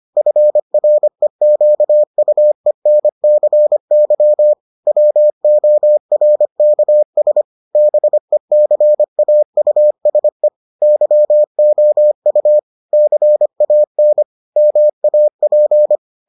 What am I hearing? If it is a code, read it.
Morse code, heard as a keyed tone, about 25 wpm, 600 Hz. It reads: FREQUENCY WORKS BECAUSE YOU CAN MAP